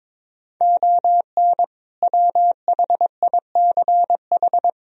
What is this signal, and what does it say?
Morse code, heard as a keyed tone, about 21 wpm, 700 Hz. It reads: ON WHICH
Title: Sentences from Top 1000 Words - Review Part 1 - 22wpm